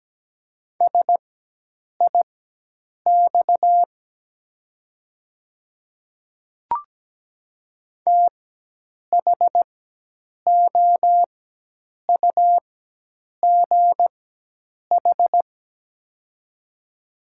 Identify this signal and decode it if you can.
Morse code, heard as a keyed tone, about 17 wpm, spaced as for 8 wpm, 700 Hz. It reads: SIX THOUGH
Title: 4x Character Spacing - Top 500 Words - 17wpm